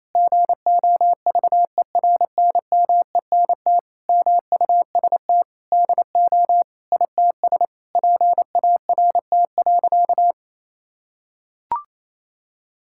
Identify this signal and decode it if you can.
Morse code, heard as a keyed tone, about 28 wpm, 700 Hz. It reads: GOVERNMENT MUST DO ITS PART.
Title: Sentences from Top 1000 Words - Review Part 5 - 28wpm